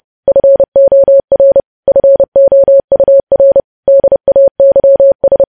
Morse code, heard as a keyed tone, about 30 wpm, 550 Hz. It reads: FOR FOUR DAYS
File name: podcast / News headlines-1x-30x30